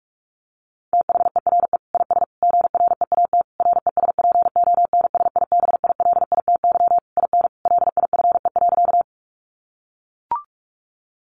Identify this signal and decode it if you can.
Morse code, heard as a keyed tone, about 44 wpm, 700 Hz. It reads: THERE IS GREAT RESPONSIBILITY IN LIFE.
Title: Sentences from Top 1200 Words 45wpm